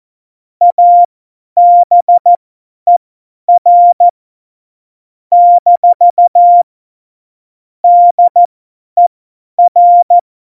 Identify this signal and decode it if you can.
Morse code, heard as a keyed tone, about 14 wpm, spaced as for 10 wpm, 700 Hz. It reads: ABER - DER
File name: Morsekurs / Witze / Witz 24. November 2020, 50 F70